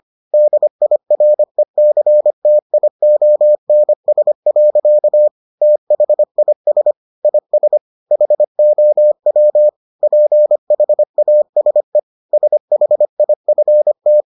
Morse code, heard as a keyed tone, about 25 wpm, 600 Hz. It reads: DIRECTIONS. THIS IS HOW PHASE SHIFT